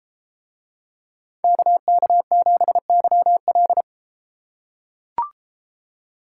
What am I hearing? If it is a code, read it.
Morse code, heard as a keyed tone, about 32 wpm, 700 Hz. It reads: KK7YL